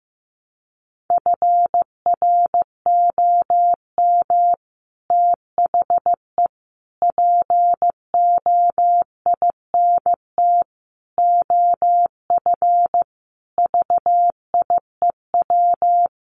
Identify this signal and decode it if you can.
Morse code, heard as a keyed tone, about 15 wpm, 700 Hz. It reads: FROM THE POINT OF VIEW
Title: Sets of 5 Words 15wpm